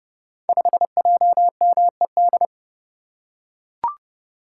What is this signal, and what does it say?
Morse code, heard as a keyed tone, about 31 wpm, 700 Hz. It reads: 5JMED
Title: Code Groups - 5 Letter:Numbers - 30wpm